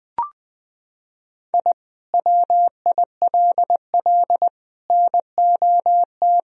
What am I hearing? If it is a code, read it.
Morse code, heard as a keyed tone, about 20 wpm, 700 Hz. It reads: I WILL NOT
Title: Sentences from Top 1000 Words - Review Part 3 - 20wpm